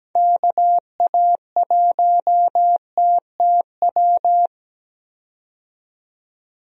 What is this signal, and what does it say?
Morse code, heard as a keyed tone, about 17 wpm, 700 Hz. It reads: KA1TTW
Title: USA Call Signs - 6 Characters - 17wpm